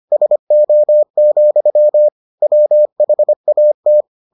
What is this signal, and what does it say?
Morse code, heard as a keyed tone, about 25 wpm, 600 Hz. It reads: SO, WHAT